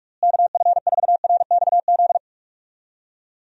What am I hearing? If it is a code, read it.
Morse code, heard as a keyed tone, about 46 wpm, 700 Hz. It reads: KF4RXZ
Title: USA Call Signs - 6 Characters - Repeated 3x - 45wpm